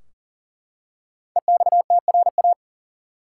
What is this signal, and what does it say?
Morse code, heard as a keyed tone, about 40 wpm, 700 Hz. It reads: EXTRA
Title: Five Letter Words - 40wpm